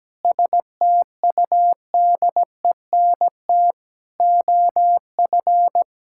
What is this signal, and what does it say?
Morse code, heard as a keyed tone, about 17 wpm, 700 Hz. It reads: STUDENT OF
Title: Sentences from Top 1000 Words - Review Part 6 - 17wpm